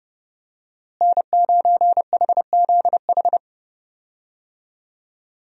Morse code, heard as a keyed tone, about 30 wpm, 700 Hz. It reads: N9HZH
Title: USA Call Signs - 5 Characters - Repeated 3x - 30wpm